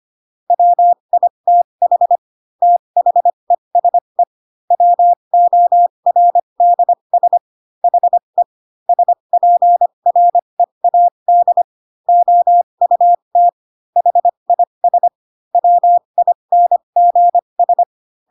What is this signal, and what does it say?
Morse code, heard as a keyed tone, about 25 wpm, spaced as for 22 wpm, 700 Hz. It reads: WITH THESE WORDS HE SPREAD OUT HIS WINGS